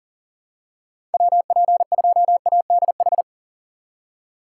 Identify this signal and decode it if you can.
Morse code, heard as a keyed tone, about 39 wpm, 700 Hz. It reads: WP2ADH
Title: USA Call Signs - 6 Characters - 3x Character Spacing - 40wpm